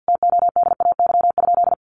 Morse code, heard as a keyed tone, about 50 wpm, 700 Hz. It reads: TODAY?
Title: Sentences from Top 1200 Words 50wpm